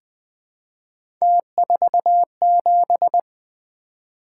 Morse code, heard as a keyed tone, about 20 wpm, 700 Hz. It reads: T47
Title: Three Letter-Number Combinations - 20wpm